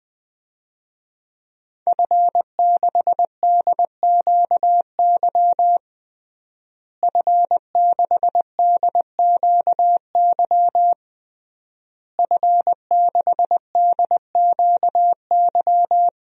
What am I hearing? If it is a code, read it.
Morse code, heard as a keyed tone, about 20 wpm, 700 Hz. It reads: F6DQY F6DQY F6DQY
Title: DXCC-100 Call Signs - 5 Characters - Repeated 3x - 20wpm